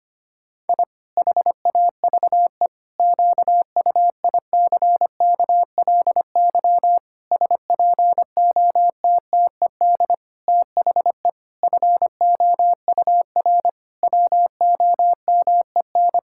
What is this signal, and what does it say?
Morse code, heard as a keyed tone, about 25 wpm, 700 Hz. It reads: I HAVE QUICKLY SPOTTED THE FOUR WOMEN
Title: Pangram Challenge v2 - 25wpm